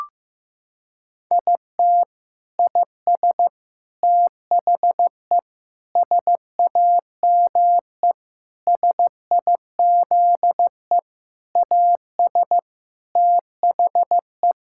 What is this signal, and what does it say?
Morse code, heard as a keyed tone, about 16 wpm, 700 Hz. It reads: IT IS THE SAME SIZE AS THE
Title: Sentences from Top 1000 Words - Review Part 6 - 15wpm